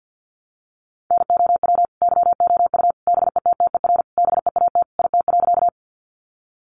Morse code, heard as a keyed tone, about 50 wpm, 700 Hz. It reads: NOW YOU BETTER BEAT IT.
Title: Sentences from Top 1300 Words 50wpm